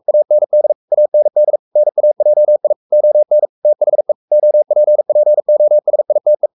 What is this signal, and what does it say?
Morse code, heard as a keyed tone, about 44 wpm, 600 Hz. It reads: AND AND NAJI ON THE OPPOSITE
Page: Romanen Never af Ken Follet på engelsk.